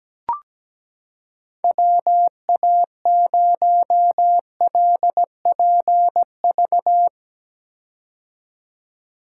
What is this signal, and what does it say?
Morse code, heard as a keyed tone, about 17 wpm, 700 Hz. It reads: WA0LPV
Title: K1USN SST QSO Practice - 17wpm